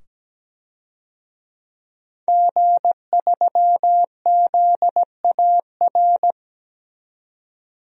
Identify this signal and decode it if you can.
Morse code, heard as a keyed tone, about 17 wpm, 700 Hz. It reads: G3ZAR
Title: DXCC-100 Call Signs - 5 Characters - 17wpm